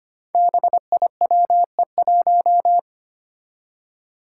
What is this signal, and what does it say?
Morse code, heard as a keyed tone, about 25 wpm, 700 Hz. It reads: BIWE1
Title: Five Challenging Letter-Number Combinations 25wpm